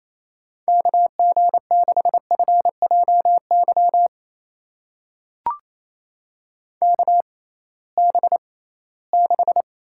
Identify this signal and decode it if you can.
Morse code, heard as a keyed tone, about 28 wpm, spaced as for 18 wpm, 700 Hz. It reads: KG6FJY KB6
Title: USA Call Signs - 6 Characters - 6x Character Spacing - 28wpm